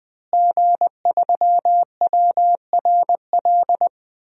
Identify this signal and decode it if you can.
Morse code, heard as a keyed tone, about 20 wpm, 700 Hz. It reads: G3WRL
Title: DXCC-100 Call Signs - 5 Characters - Repeated 2x - 20wpm